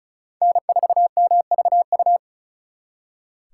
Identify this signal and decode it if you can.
Morse code, heard as a keyed tone, about 43 wpm, 700 Hz. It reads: N4MVU
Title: Speed Racing - USA 1x3 Call Signs 35wpm